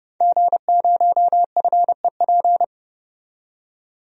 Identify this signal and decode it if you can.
Morse code, heard as a keyed tone, about 30 wpm, 700 Hz. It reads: G0FEP
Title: DXCC-100 Call Signs - 5 Characters - 30wpm